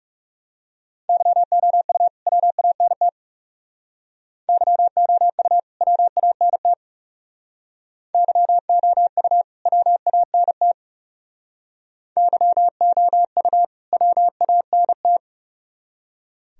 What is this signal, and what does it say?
Morse code, heard as a keyed tone, about 37 wpm, 700 Hz. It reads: YOU WANT YOU WANT YOU WANT YOU WANT
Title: Speed-Racing - Sets of 2 Words 30wpm